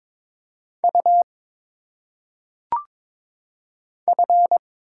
Morse code, heard as a keyed tone, about 22 wpm, 700 Hz. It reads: U F
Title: Single Letters - Phonetic Pronunciation 22wpm